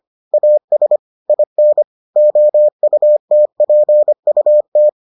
Morse code, heard as a keyed tone, about 25 wpm, 600 Hz. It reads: AS IN OUTPUT